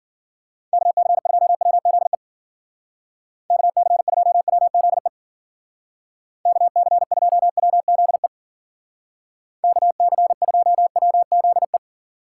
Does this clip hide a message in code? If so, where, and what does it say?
Yes: Morse code from 0.7 to 11.8 s, heard as a keyed tone, about 51 wpm, 700 Hz. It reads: KC2WZE KC2WZE KC2WZE KC2WZE